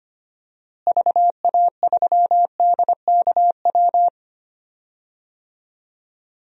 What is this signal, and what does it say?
Morse code, heard as a keyed tone, about 25 wpm, 700 Hz. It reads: VA3DKW
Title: DXCC-100 Call Signs - 6 Characters - Repeated 3x - 25wpm